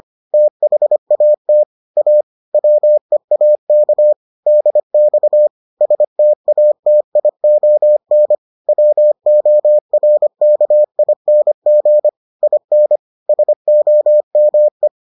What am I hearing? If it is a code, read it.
Morse code, heard as a keyed tone, about 25 wpm, 600 Hz. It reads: THAT A WEAK DX STATION WORKING IN SOME